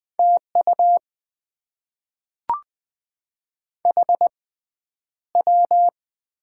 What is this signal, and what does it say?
Morse code, heard as a keyed tone, about 20 wpm, spaced as for 10 wpm, 690 Hz. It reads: TU HW
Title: Top 20 QSO Elements - 6x Character Spacing - 20wpm